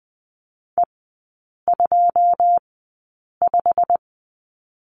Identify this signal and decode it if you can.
Morse code, heard as a keyed tone, about 20 wpm, spaced as for 8 wpm, 700 Hz. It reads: E25
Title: 145 Lesson 24e - Letter y - Three Characters 20wpm